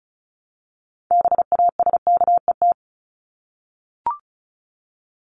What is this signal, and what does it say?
Morse code, heard as a keyed tone, about 34 wpm, 700 Hz. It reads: BASKET
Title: Top 1751-2000 Words 35wpm